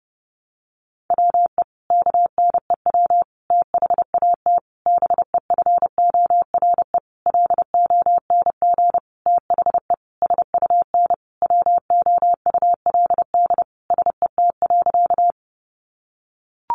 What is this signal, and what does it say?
Morse code, heard as a keyed tone, about 30 wpm, 700 Hz. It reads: WE KNEW THAT BEFORE LONG THE SUN WOULD SET.
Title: Sentences from Top 400 Words 30wpm